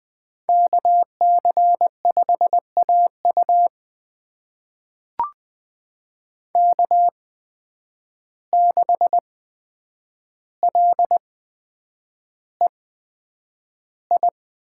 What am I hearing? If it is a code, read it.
Morse code, heard as a keyed tone, about 20 wpm, spaced as for 9 wpm, 700 Hz. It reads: KC5AU K6LEI
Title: USA Call Signs - 5 Characters - 8x Character Spacing - 20wpm